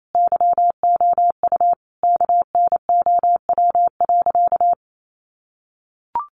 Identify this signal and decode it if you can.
Morse code, heard as a keyed tone, about 28 wpm, 700 Hz. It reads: YOU KNOW.
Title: Sentences from Top 1200 Words 28wpm